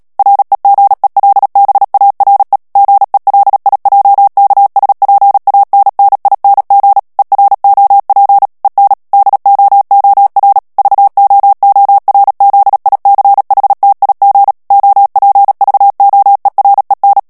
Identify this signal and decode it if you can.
Morse code, heard as a keyed tone, about 37 wpm, 800 Hz. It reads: REGELBARE GELIJKSPANNING EROP EN DOOR VOORZICHTIG OPVOEREN
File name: morse mp3 / HOLLANDAIS texte clair 2 vitesse 35 mots minute